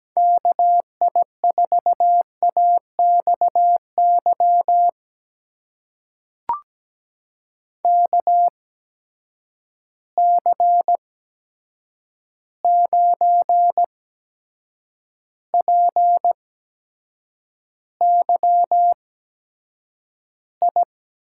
Morse code, heard as a keyed tone, about 17 wpm, spaced as for 7 wpm, 700 Hz. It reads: KI4AXY KC9PYI
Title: USA Call Signs - 6 Characters - 8x Character Spacing - 17wpm